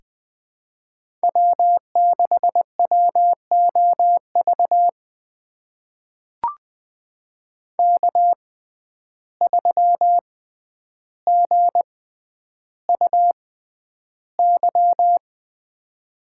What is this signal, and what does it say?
Morse code, heard as a keyed tone, about 20 wpm, spaced as for 10 wpm, 700 Hz. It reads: W6WOV K3GUY
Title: USA Call Signs - 5 Characters - 6x Character Spacing - 20wpm